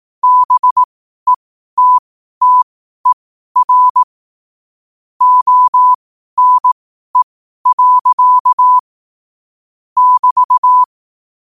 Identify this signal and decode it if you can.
Morse code, heard as a keyed tone, about 18 wpm, spaced as for 12 wpm, 1000 Hz. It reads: BETTER ONE. <BT>
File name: qotdfiles / 12wpm